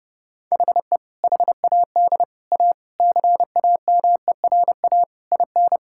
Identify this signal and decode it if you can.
Morse code, heard as a keyed tone, about 30 wpm, 700 Hz. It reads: HE HAD A CAMERA IN
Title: Sentences from Top 1000 Words - Review Part 5 - 30wpm